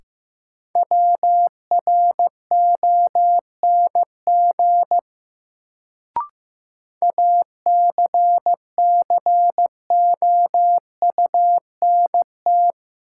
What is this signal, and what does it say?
Morse code, heard as a keyed tone, about 15 wpm, 700 Hz. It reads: WRONG ACCOUNT
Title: Sentences from Top 900 Words - New Word Drill - 15wpm